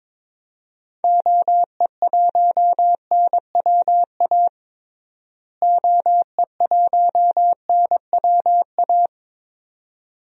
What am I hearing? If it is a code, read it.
Morse code, heard as a keyed tone, about 22 wpm, 700 Hz. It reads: OE1NWA OE1NWA
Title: DXCC-100 Call Signs - 6 Characters - Repeated 2x - 22wpm